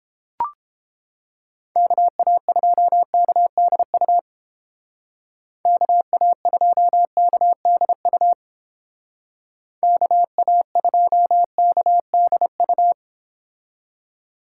Morse code, heard as a keyed tone, about 30 wpm, 700 Hz. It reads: KA2KDU KA2KDU KA2KDU